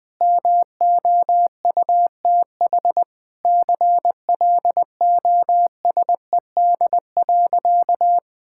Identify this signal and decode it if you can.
Morse code, heard as a keyed tone, about 20 wpm, 700 Hz. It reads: MOUTH CLOSED.
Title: Easy Sentences with Hints - 20wpm